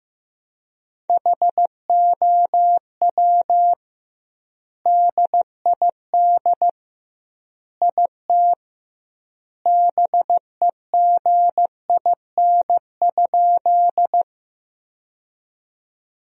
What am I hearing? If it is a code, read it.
Morse code, heard as a keyed tone, about 16 wpm, 700 Hz. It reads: HOW DID IT BEGIN?
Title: 2x Word-Spacing - Sentences from Top 300 Words - 15wpm